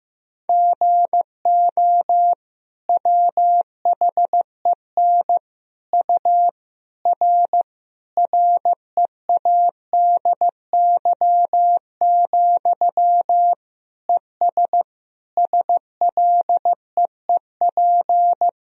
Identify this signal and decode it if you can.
Morse code, heard as a keyed tone, about 15 wpm, 700 Hz. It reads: GO WHEN U R READY, ES SLEEP
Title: Sentences from Top 500 Words and Abbreviations 15wpm